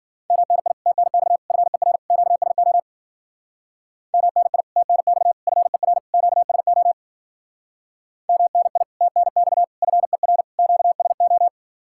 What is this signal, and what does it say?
Morse code, heard as a keyed tone, about 50 wpm, 700 Hz. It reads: MNI TNX FER QSO MNI TNX FER QSO MNI TNX FER QSO
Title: Speed-Racing Common QSO Phrases - 40wpm